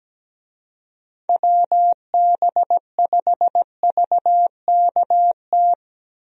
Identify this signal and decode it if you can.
Morse code, heard as a keyed tone, about 17 wpm, 700 Hz. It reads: WB5VKT
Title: USA Call Signs - 6 Characters - Repeated 3x - 17wpm